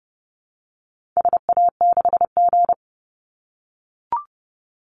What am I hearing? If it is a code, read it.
Morse code, heard as a keyed tone, about 30 wpm, 700 Hz. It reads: SA6G
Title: Call Signs - 30wpm